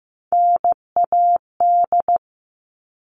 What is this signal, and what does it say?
Morse code, heard as a keyed tone, about 15 wpm, 700 Hz. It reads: NAD
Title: All Cut Numbers - 3-Digit 15wpm